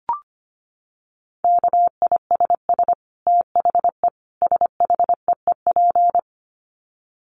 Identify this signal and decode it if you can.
Morse code, heard as a keyed tone, about 25 wpm, 700 Hz. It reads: KISS THE SHEEP
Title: Dit Challenge 25wpm